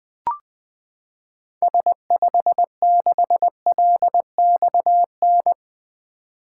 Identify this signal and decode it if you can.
Morse code, heard as a keyed tone, about 21 wpm, 700 Hz. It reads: S56LXN
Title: 2023 SOTA Activator Call Signs - 20wpm